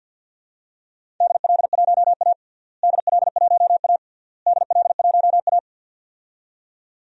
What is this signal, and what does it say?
Morse code, heard as a keyed tone, about 48 wpm, 700 Hz. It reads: DL1A DL1A DL1A
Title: DXCC-100 Call Signs - 4 Characters - Repeated 3x - 50wpm